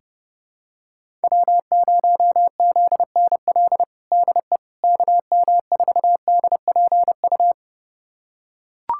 Morse code, heard as a keyed tone, about 30 wpm, 700 Hz. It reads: W0ZNL DE KM4DPU
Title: USA Callsign DE Callsign 30wpm